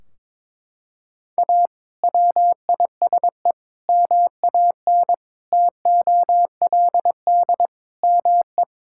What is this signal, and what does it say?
Morse code, heard as a keyed tone, about 22 wpm, 700 Hz. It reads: A WISE MAN TOLD ME
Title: Sentences from Top 1600 Words 22wpm